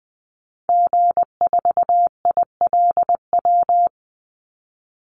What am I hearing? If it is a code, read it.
Morse code, heard as a keyed tone, about 20 wpm, 700 Hz. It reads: G4ILW
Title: Call Signs Repeated 2x - 20wpm